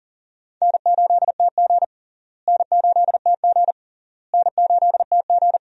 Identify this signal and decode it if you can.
Morse code, heard as a keyed tone, about 40 wpm, 700 Hz. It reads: N8TG N8TG N8TG
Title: USA Call Signs - 4 Characters - Repeated 3x - 40wpm